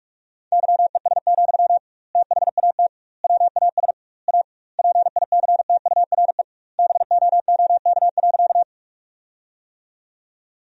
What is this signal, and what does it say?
Morse code, heard as a keyed tone, about 46 wpm, 700 Hz. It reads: YES, THAT WAS A PICTURE BOOK.
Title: 2x Word-Spacing - Sentences from Top 500 Words - 45wpm